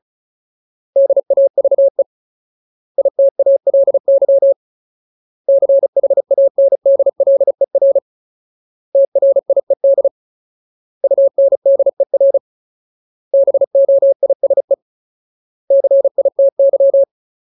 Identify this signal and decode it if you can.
Morse code, heard as a keyed tone, about 34 wpm, 550 Hz. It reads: DAVE ITALY CHANDLER TRIED UNDER BOISE CITY
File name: Ultimate Word List / 35WPM Ultimate